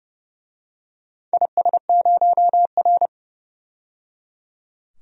Morse code, heard as a keyed tone, about 30 wpm, 700 Hz. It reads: IS0R
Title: DXCC-100 Call Signs - 4 Characters - 30wpm